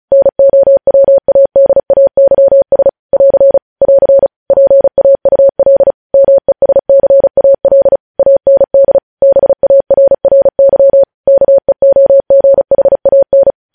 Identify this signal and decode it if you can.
Morse code, heard as a keyed tone, about 36 wpm, 550 Hz. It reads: NOWADAYS <AR> <AR> PAUL MESCAL AND BARRY KEOGHAN
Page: news headlines in morse code at 35 wpm